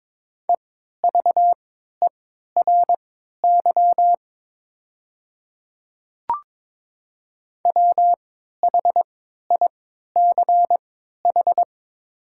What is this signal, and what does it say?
Morse code, heard as a keyed tone, about 22 wpm, spaced as for 12 wpm, 700 Hz. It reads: EVERY WHICH
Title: 3x Character Spacing - Top 200 Words - 22wpm